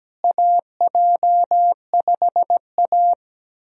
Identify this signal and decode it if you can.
Morse code, heard as a keyed tone, about 17 wpm, 700 Hz. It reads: AJ5A
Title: USA Call Signs - 4 Characters - Repeated 2x - 17wpm